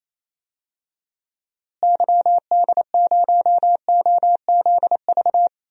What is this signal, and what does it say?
Morse code, heard as a keyed tone, about 28 wpm, 700 Hz. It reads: YD0OZV
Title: DXCC-100 Call Signs - 6 Characters - Repeated 2x - 28wpm